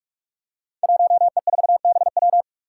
Morse code, heard as a keyed tone, about 44 wpm, 700 Hz. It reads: 1E4BW
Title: Five Challenging Letter-Number Combinations 45wpm